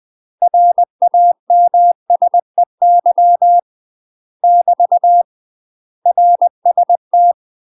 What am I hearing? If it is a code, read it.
Morse code, heard as a keyed tone, about 20 wpm, 700 Hz. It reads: RAMSEY <BT> RST
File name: UK MASTER QSOs TRAINING / 20WPM / UK MASTER QSOs 0062